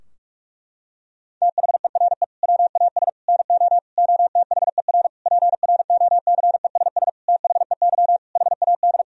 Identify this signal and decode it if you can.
Morse code, heard as a keyed tone, about 46 wpm, 700 Hz. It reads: THERE WAS NO OTHER PROCESS THEY HAD
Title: Sentences from Top 1000 Words - Review Part 3 - 45wpm